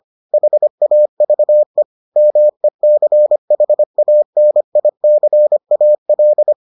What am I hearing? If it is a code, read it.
Morse code, heard as a keyed tone, about 25 wpm, 600 Hz. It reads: HAVE MECHANICAL